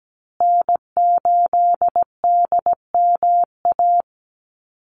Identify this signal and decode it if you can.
Morse code, heard as a keyed tone, about 17 wpm, 700 Hz. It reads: N8DMA
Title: USA Call Signs Repeated 3x 17wpm